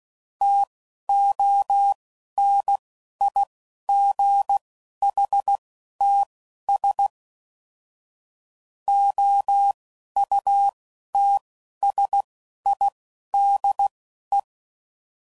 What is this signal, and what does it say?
Morse code, heard as a keyed tone, about 16 wpm, spaced as for 11 wpm, 800 Hz. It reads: TONIGHTS OUTSIDE
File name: CW Code Groups @10WPM/Weather Sets